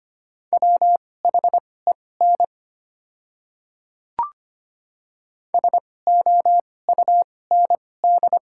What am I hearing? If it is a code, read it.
Morse code, heard as a keyed tone, about 25 wpm, spaced as for 18 wpm, 690 Hz. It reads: WHEN SOUND
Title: 2x Character Spacing - Top 100 Words - 25wpm